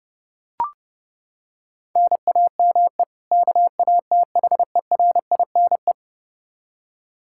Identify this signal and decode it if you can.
Morse code, heard as a keyed tone, about 30 wpm, 700 Hz. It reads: NAME KATHERINE
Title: Practice QSO - Exchange 1 - Beginning - 30wpm